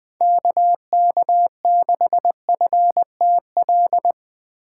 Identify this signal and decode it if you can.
Morse code, heard as a keyed tone, about 20 wpm, 700 Hz. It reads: KK6FTL
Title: Speed Racing - USA 2x3 Call Signs 17wpm